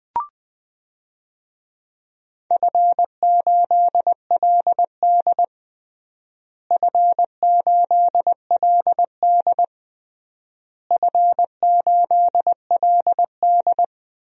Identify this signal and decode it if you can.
Morse code, heard as a keyed tone, about 19 wpm, 700 Hz. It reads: F8LD F8LD F8LD
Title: DXCC-100 Call Signs - 4 Characters - Repeated 3x - 20wpm